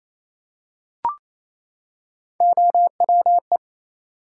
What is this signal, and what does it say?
Morse code, heard as a keyed tone, about 28 wpm, 700 Hz. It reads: OWE
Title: Three Letter Words - 28wpm